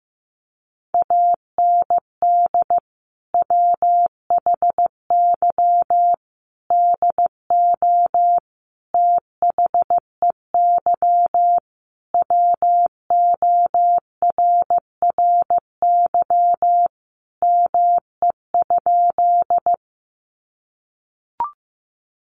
Morse code, heard as a keyed tone, about 15 wpm, 700 Hz. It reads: AND WHY DO THEY WORRY ME?
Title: Sentences from Top 1100 Words 15wpm